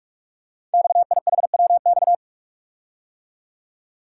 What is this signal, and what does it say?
Morse code, heard as a keyed tone, about 44 wpm, 690 Hz. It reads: XIHWX
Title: Code Groups - 5 Letters - 45wpm